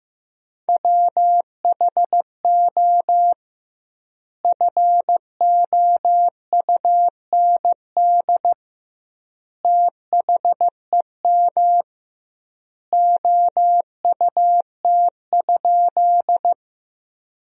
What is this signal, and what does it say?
Morse code, heard as a keyed tone, about 15 wpm, 700 Hz. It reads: WHO FOUND THEM OUT?
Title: 2x Word-Spacing - Sentences from Top 300 Words - 15wpm